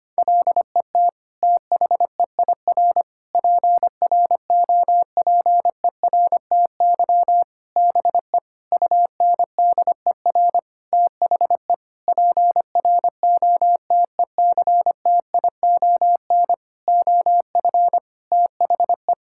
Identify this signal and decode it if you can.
Morse code, heard as a keyed tone, about 25 wpm, 700 Hz. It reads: LET THEIR PROPERTY BE UNDER THE PROTECTION OF THE
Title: Sentences from Top 1600 Words 25wpm